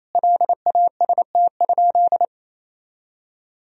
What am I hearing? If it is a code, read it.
Morse code, heard as a keyed tone, about 28 wpm, 700 Hz. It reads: LAST?
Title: Sentences from Top 1000 Words - Review Part 4 - 28wpm